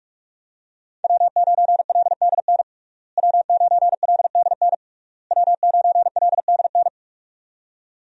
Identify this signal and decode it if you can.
Morse code, heard as a keyed tone, about 46 wpm, 700 Hz. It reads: W9LDN W9LDN W9LDN
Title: USA Call Signs - 5 Characters - Repeated 3x - 45wpm